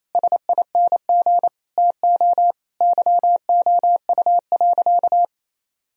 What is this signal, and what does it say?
Morse code, heard as a keyed tone, about 27 wpm, 700 Hz. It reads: SING TO YOU.
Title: Sentences from Top 1000 Words - Review Part 2 - 28wpm